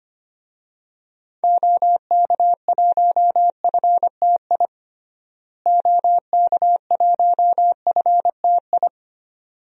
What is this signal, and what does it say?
Morse code, heard as a keyed tone, about 25 wpm, 700 Hz. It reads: OK1FTI OK1FTI
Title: DXCC-100 Call Signs - 6 Characters - Repeated 2x - 25wpm